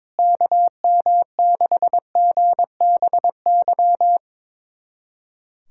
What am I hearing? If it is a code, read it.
Morse code, heard as a keyed tone, about 22 wpm, 700 Hz. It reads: KM6GBY
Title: USA Callsign DE Callsign - 6 Characters - 22wpm